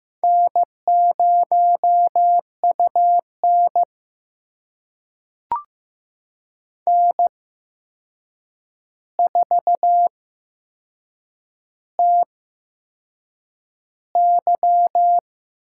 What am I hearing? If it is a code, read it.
Morse code, heard as a keyed tone, about 15 wpm, spaced as for 6 wpm, 700 Hz. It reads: N0UN N4TY
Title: USA Call Signs - 4 Characters - 8x Character Spacing - 15wpm